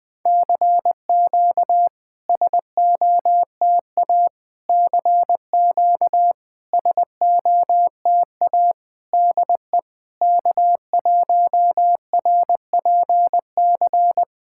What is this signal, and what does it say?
Morse code, heard as a keyed tone, about 20 wpm, 690 Hz. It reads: CQ SOTA CQ SOTA DE K1RPC
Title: SOTA QSO Practice 20wpm